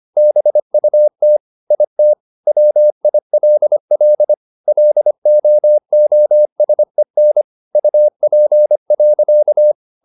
Morse code, heard as a keyed tone, about 25 wpm, 600 Hz. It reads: BUT IT WILL LOOSEN UP.
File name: the old man and the sea part 1 25wpm